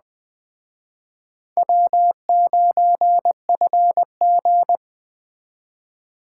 Morse code, heard as a keyed tone, about 20 wpm, 700 Hz. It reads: W9FG